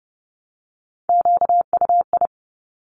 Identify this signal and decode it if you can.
Morse code, heard as a keyed tone, about 30 wpm, 690 Hz. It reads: QUI
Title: Top Three Letter Combinations in Words 30wpm